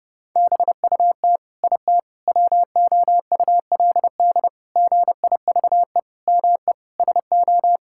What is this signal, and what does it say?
Morse code, heard as a keyed tone, about 30 wpm, 700 Hz. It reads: BUT IT WOULD GIVE ME SO
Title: Sentences from Top 1600 Words 30wpm